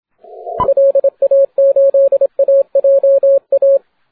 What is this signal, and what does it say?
Morse code, heard as a keyed tone, about 31 wpm, 590 Hz, with radio noise. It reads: LA8AJA
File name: wp / n1snb cy0 40m cw 07 mar 2011 0614z